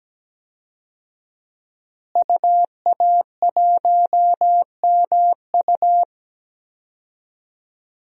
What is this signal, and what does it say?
Morse code, heard as a keyed tone, about 17 wpm, 700 Hz. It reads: UA1MU
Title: DXCC-100 Call Signs - 5 Characters - Repeated 3x - 17wpm